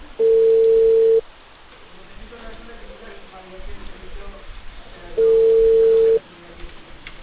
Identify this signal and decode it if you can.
Ringback tone (China, 450 Hz), with speech.